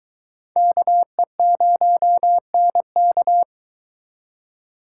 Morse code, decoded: KE0NK

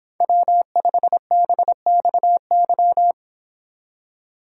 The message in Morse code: W5BXY

